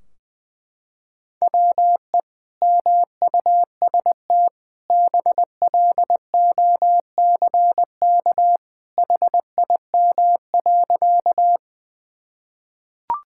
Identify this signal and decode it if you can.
Morse code, heard as a keyed tone, about 20 wpm, 700 Hz. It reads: WE MUST BLOCK HIM.
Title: Sentences from Top 1600 Words 20wpm